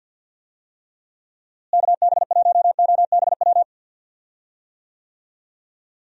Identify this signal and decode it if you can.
Morse code, heard as a keyed tone, about 46 wpm, 700 Hz. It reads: KB1OBW
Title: USA Call Signs - 6 Characters - Repeated 3x - 50wpm